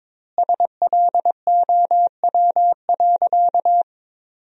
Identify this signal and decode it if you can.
Morse code, heard as a keyed tone, about 22 wpm, 700 Hz. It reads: SLOW.
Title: 2x Word-Spacing - Sentences from Top 400 Words - 22wpm